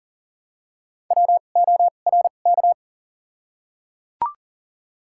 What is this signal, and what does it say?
Morse code, heard as a keyed tone, about 37 wpm, spaced as for 28 wpm, 700 Hz. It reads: WORK
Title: 2x Character Spacing - Top 300 Words - 40wpm